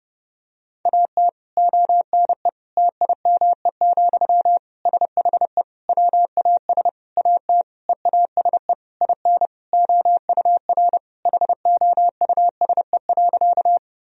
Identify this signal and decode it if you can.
Morse code, heard as a keyed tone, about 30 wpm, 700 Hz. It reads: AT ONE TIME, SHE WAS AT EASE IN OUR HOUSE.
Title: Sentences from Top 1000 Words - Review Part 1 - 30wpm